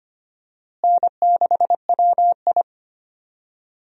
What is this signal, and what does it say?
Morse code, heard as a keyed tone, about 25 wpm, 700 Hz. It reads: N6WI
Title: USA Call Signs - 4 Characters - Repeated 3x - 25wpm